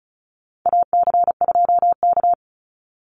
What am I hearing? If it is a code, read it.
Morse code, heard as a keyed tone, about 34 wpm, 700 Hz. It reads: AC2K
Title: Call Signs Repeated 2x - 35wpm